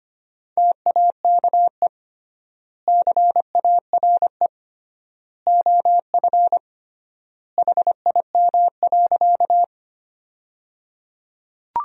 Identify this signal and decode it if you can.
Morse code, heard as a keyed tone, about 25 wpm, 700 Hz. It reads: TAKE CARE OF HIM.
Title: 3x Word-Spacing - Sentences from Top 300 Words - 25wpm